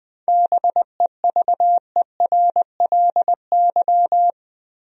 Morse code, heard as a keyed tone, about 20 wpm, 700 Hz. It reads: BEVERLY
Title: USA Common Names - 20wpm